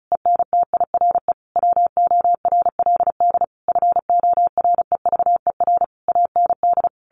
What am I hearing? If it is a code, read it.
Morse code, heard as a keyed tone, about 34 wpm, 700 Hz. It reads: ENTIRE WORLD FOREVER AND
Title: A Sound Of Thunder 35wpm